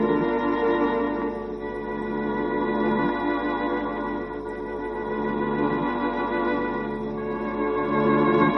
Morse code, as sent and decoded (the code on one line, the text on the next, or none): none